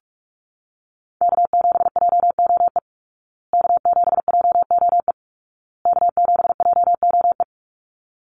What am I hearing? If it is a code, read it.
Morse code, heard as a keyed tone, about 44 wpm, 700 Hz. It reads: K7JOE K7JOE K7JOE
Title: Call Signs Repeated 2x - 45wpm